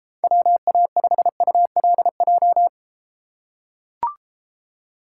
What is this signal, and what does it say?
Morse code, heard as a keyed tone, about 32 wpm, 700 Hz. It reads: WA5ULJ